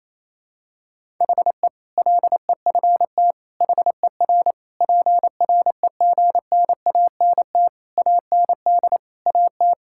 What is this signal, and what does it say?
Morse code, heard as a keyed tone, about 27 wpm, 700 Hz. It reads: HE LEFT HER PREGNANT AND AT
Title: Sentences from Top 1000 Words - Review Part 6 - 28wpm